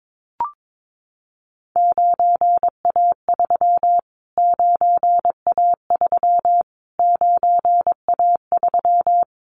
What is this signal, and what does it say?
Morse code, heard as a keyed tone, about 22 wpm, 700 Hz. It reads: 9A3 9A3 9A3
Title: Callsign Prefixes for Top 25 DXCC - Repeated 3x - 22wpm